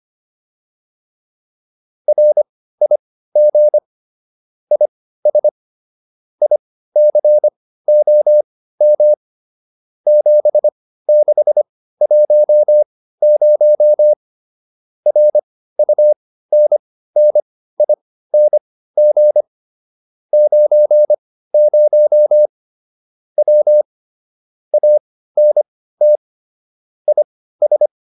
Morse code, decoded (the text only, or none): RIG IS ICOM 7610 RUNNING 90 W ANT IS